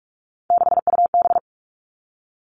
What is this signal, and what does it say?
Morse code, heard as a keyed tone, about 44 wpm, 700 Hz. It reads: 6UB
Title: All Cut Numbers - 3-Digit 45wpm